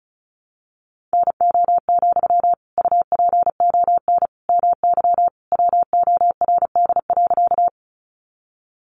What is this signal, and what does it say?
Morse code, heard as a keyed tone, about 34 wpm, 700 Hz. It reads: NO, UPON MY WORD.
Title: Sentences from Top 1100 Words 35wpm